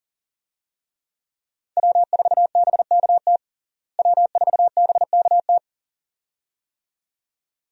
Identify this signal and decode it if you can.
Morse code, heard as a keyed tone, about 40 wpm, 700 Hz. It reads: W4BKT W4BKT
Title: USA Call Signs - 5 Characters - Repeated 2x - 40wpm